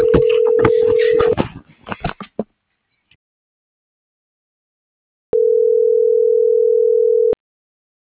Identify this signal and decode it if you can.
Ringback tone (North America), 440 + 480 Hz.